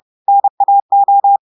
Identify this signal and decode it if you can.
Morse code, heard as a keyed tone, about 30 wpm, 800 Hz. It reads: NAO
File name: audios / portugais / portugais texte en clair 2 vitesse 30 mots minutes